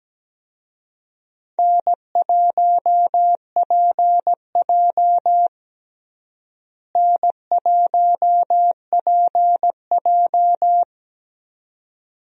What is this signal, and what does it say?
Morse code, heard as a keyed tone, about 17 wpm, 700 Hz. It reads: N1PJ N1PJ
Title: USA Call Signs - 4 Characters - Repeated 2x - 17wpm